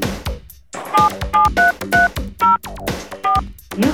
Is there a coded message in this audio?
Yes: DTMF from 0.9 to 3.4 s, heard as telephone touch tones, tones of about 160 ms. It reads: 77330 4